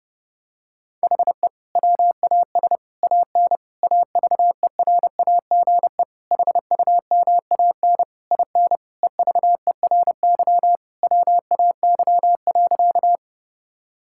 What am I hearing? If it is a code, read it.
Morse code, heard as a keyed tone, about 30 wpm, 700 Hz. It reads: HE WAS AN AVERAGE HUMAN IN EVERY WAY.
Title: Sentences from Top 1000 Words - Review Part 6 - 30wpm